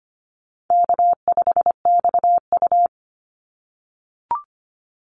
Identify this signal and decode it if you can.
Morse code, heard as a keyed tone, about 25 wpm, 700 Hz. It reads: K5XU
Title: Call Signs - 25wpm